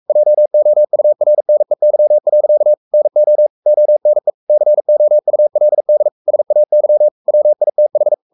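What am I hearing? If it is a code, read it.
Morse code, heard as a keyed tone, about 44 wpm, 600 Hz. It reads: JOURNEY. NO ONE COULD SAY WITH
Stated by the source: Morse code at 43 wpm sending Ken Follett's novel Never in English, from CW med Gnister.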